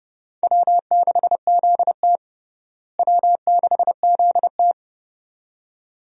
Morse code, heard as a keyed tone, about 30 wpm, 700 Hz. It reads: W6ZT W6ZT
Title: USA Call Signs - 4 Characters - Repeated 2x - 30wpm